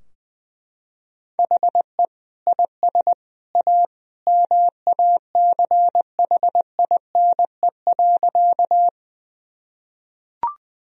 Morse code, heard as a keyed tone, about 21 wpm, 700 Hz. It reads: HE IS A MACHINE.
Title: Sentences from Top 1000 Words - Review Part 2 - 20wpm